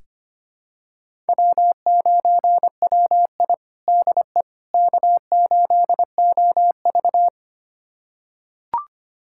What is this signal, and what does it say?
Morse code, heard as a keyed tone, about 25 wpm, 700 Hz. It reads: W9WI DE K8OV
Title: USA Callsign DE Callsign - 4 Characters - 25wpm